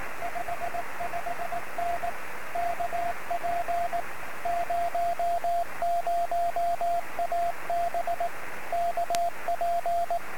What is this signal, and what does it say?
Morse code, heard as a keyed tone, about 18 wpm, 690 Hz, with radio noise. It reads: 55N KP00AB KP
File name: sm7lcb / 01-oh0jfp-23cm